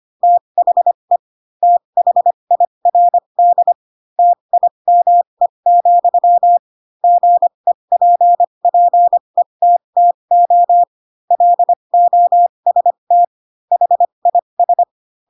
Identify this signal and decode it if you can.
Morse code, heard as a keyed tone, about 25 wpm, spaced as for 22 wpm, 700 Hz. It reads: THE THIRD TIME, GEPPETTO LOST HIS
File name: Pinocchio / Pinocchio ch2 18 25 ik1hge